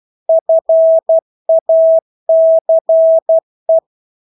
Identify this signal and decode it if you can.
Morse code, heard as a keyed tone, about 12 wpm, 650 Hz. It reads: FACE
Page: Engelske ord 12 wpm